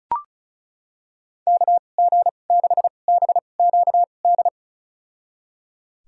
Morse code, heard as a keyed tone, about 34 wpm, spaced as for 25 wpm, 700 Hz. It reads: KG6BQD
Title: USA Call Signs - 6 Characters - 2x Character Spacing - 35wpm